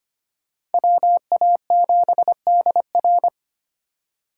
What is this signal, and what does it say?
Morse code, heard as a keyed tone, about 32 wpm, 700 Hz. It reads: WA7DR